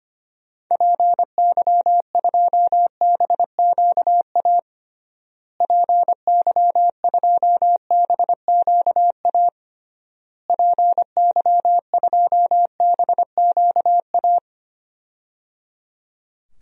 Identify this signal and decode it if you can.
Morse code, heard as a keyed tone, about 25 wpm, 700 Hz. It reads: PY2BQA PY2BQA PY2BQA
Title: DXCC-100 Call Signs - 6 Characters - Repeated 3x - 25wpm